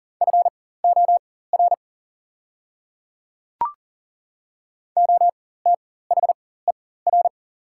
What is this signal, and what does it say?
Morse code, heard as a keyed tone, about 40 wpm, spaced as for 19 wpm, 700 Hz. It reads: FOR OTHER
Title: 4x Character Spacing - Top 100 Words - 40wpm